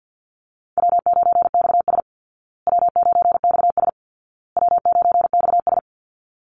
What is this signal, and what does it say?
Morse code, heard as a keyed tone, about 49 wpm, 700 Hz. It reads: W9XS W9XS W9XS
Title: Call Signs Repeated 2x - 50wpm